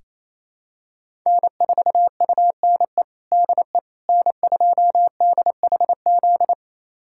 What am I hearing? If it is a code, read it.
Morse code, heard as a keyed tone, about 28 wpm, 700 Hz. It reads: N4UNE DE N2DHZ
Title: USA Callsign DE Callsign - 5 Characters - 28wpm